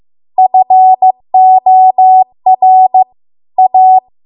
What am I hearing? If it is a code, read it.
Morse code, heard as a keyed tone, about 16 wpm, 750 Hz. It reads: FOR A